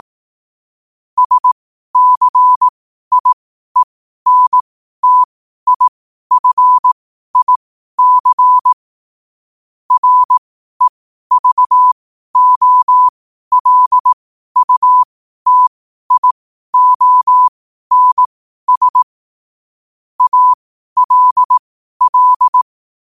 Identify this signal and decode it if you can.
Morse code, heard as a keyed tone, about 18 wpm, spaced as for 12 wpm, 1000 Hz. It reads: SCIENTIFIC REVOLUTIONS ALL
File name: qotdfiles / 12wpm